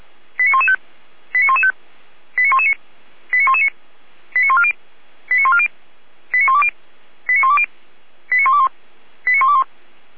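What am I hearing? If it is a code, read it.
Five-tone selective call (ZVEI), heard as quick runs of five tones, tones of about 70 ms. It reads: 89115 89115 89119 89119 89140 89140 89120 89120 89121 89121
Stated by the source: ZVEI five-tone call received by SDR, logged by Feuerwehr Orlamünde (fire brigade).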